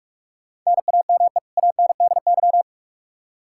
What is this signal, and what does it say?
Morse code, heard as a keyed tone, about 42 wpm, 700 Hz. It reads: NAME ANDY
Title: Practice QSO - Exchange 1 - Beginning - 45wpm